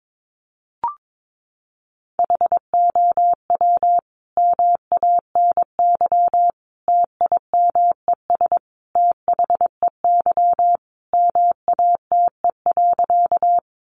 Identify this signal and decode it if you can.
Morse code, heard as a keyed tone, about 22 wpm, 700 Hz. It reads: HOW MANY TIMES THEY MATE.
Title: A Sound Of Thunder 22wpm